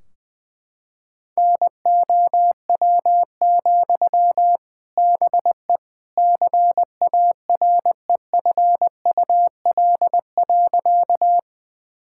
Morse code, heard as a keyed tone, about 21 wpm, 700 Hz. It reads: NOW, BE CAREFUL.
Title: Sentences from Top 1600 Words 20wpm